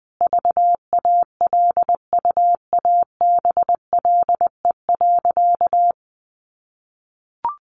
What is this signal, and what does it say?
Morse code, heard as a keyed tone, about 20 wpm, 700 Hz. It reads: VALUABLE.
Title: Sentences from Top 1100 Words 20wpm